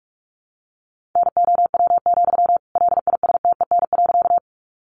Morse code, heard as a keyed tone, about 44 wpm, 700 Hz. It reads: NOW, LISTEN.
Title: Sentences from Top 400 Words 45wpm